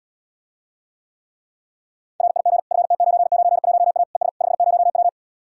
Morse code, heard as a keyed tone, about 79 wpm, 700 Hz. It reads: RIG FT991A ES 50W